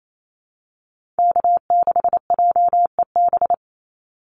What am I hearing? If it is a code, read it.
Morse code, heard as a keyed tone, about 28 wpm, 690 Hz. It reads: K6JEB
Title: Call Signs Repeated 3x - 28wpm